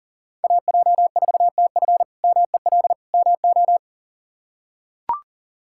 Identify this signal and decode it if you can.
Morse code, heard as a keyed tone, about 40 wpm, 700 Hz. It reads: AJ4TF MEL MO
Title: K1USN SST QSO Practice - 40wpm